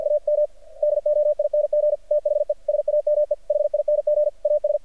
Morse code, heard as a keyed tone, about 51 wpm, 600 Hz. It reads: AM DOING THE SAME HING AS
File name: archived / CRSnet